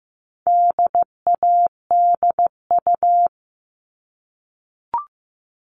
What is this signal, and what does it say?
Morse code, heard as a keyed tone, about 15 wpm, 700 Hz. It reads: DADU
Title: All Cut Numbers - 4-Digit 15wpm